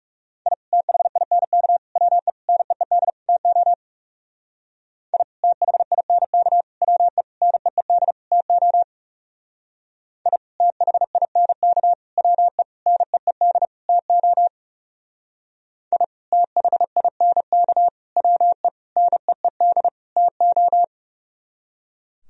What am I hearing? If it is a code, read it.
Morse code, heard as a keyed tone, about 38 wpm, 700 Hz. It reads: I THINK WE NEED TO I THINK WE NEED TO I THINK WE NEED TO I THINK WE NEED TO
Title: Speed-Racing - Sets of 5 Words 30wpm